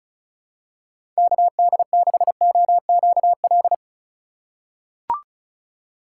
Morse code, heard as a keyed tone, about 34 wpm, 700 Hz. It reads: KD6OQL
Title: USA Call Signs - 6 Characters - 4x Character Spacing - 35wpm